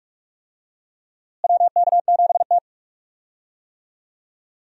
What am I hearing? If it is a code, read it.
Morse code, heard as a keyed tone, about 42 wpm, 700 Hz. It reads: WK7T
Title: USA Call Signs - 4 Characters - Repeated 2x - 45wpm